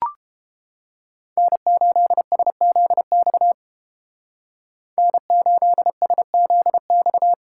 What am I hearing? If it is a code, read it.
Morse code, heard as a keyed tone, about 30 wpm, 700 Hz. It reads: N8SZX N8SZX